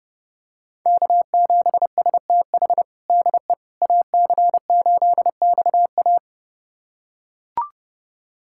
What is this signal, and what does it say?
Morse code, heard as a keyed tone, about 30 wpm, 700 Hz. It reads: K7STH DE AC8XA
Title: USA Callsign DE Callsign - 5 Characters - 30wpm